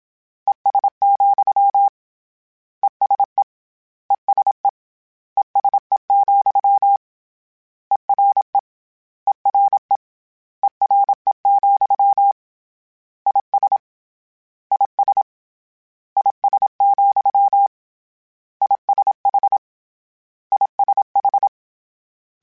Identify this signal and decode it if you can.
Morse code, heard as a keyed tone, about 27 wpm, 800 Hz. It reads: ES, ESE ESE ESE, ERE ERE ERE, IS IS IS, ISH ISH
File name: General / General-mp3s / General @25